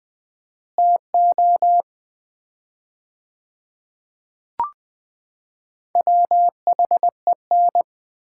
Morse code, heard as a keyed tone, about 21 wpm, 700 Hz. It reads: TO WHEN
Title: 4x Word-Spacing - Sets of 4 Words - 20wpm